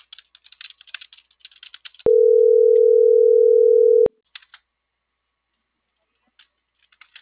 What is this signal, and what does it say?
Ringback tone (North America), 440 + 480 Hz.